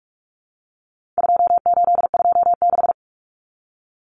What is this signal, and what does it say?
Morse code, heard as a keyed tone, about 44 wpm, 700 Hz. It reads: 2826
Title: Numbers - 4-Digit 45wpm